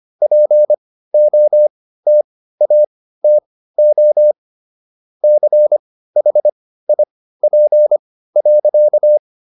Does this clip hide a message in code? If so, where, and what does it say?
Yes: Morse code from 0.2 to 9.2 s, heard as a keyed tone, about 25 wpm, spaced as for 15 wpm, 600 Hz. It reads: POTATO CHIP.